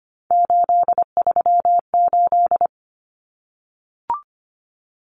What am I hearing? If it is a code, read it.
Morse code, heard as a keyed tone, about 25 wpm, 700 Hz. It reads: 838
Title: Numbers - 3-Digit 25wpm